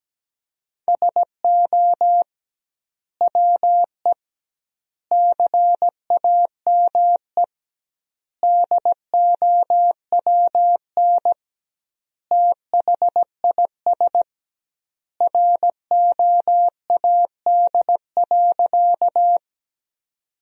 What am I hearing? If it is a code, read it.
Morse code, heard as a keyed tone, about 17 wpm, 700 Hz. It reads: SO WE CAME DOWN THIS ROAD.
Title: 2x Word-Spacing - Sentences from Top 400 Words - 17wpm